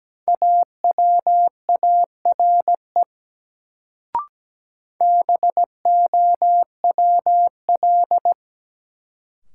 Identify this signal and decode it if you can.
Morse code, heard as a keyed tone, about 17 wpm, 700 Hz. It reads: AWARE BOWL
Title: Sentences from Top 1200 Words - New Word Drill - 17wpm